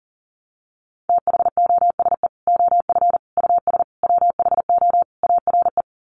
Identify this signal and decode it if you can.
Morse code, heard as a keyed tone, about 40 wpm, 700 Hz. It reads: THOSE OF US WHO ARE
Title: Sets of 5 Words Encore2 40wpm